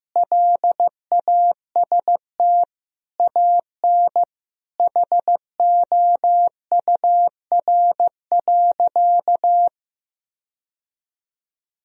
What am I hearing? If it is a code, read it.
Morse code, heard as a keyed tone, about 15 wpm, 700 Hz. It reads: LAST AN HOUR.
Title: Sentences from Top 1000 Words - Review Part 1 - 15wpm